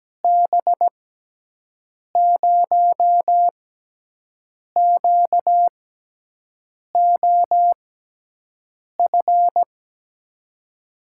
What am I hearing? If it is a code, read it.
Morse code, heard as a keyed tone, about 17 wpm, spaced as for 6 wpm, 700 Hz. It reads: B0QOF
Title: USA Call Signs - 6x Character Spacing - 17wpm